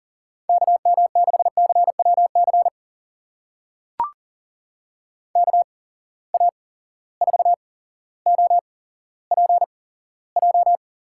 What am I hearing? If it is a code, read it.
Morse code, heard as a keyed tone, about 40 wpm, spaced as for 17 wpm, 700 Hz. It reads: KM6CWC KA4OPJ
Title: USA Call Signs - 6 Characters - 8x Character Spacing - 40wpm